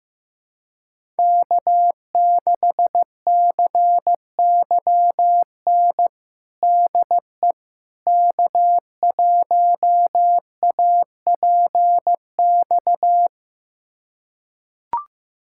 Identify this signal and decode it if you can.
Morse code, heard as a keyed tone, about 15 wpm, 700 Hz. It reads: K6CYN DE K1APX
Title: USA Callsign DE Callsign - 5 Characters - 15wpm